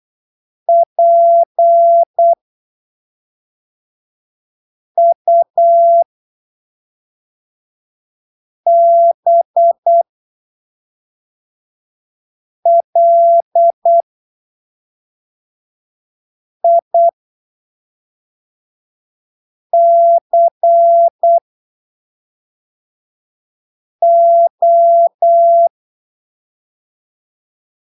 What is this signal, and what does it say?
Morse code, heard as a keyed tone, about 8 wpm, spaced as for 3 wpm, 680 Hz. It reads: PUBLICO
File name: arquivos / texto / texto 01 08ppm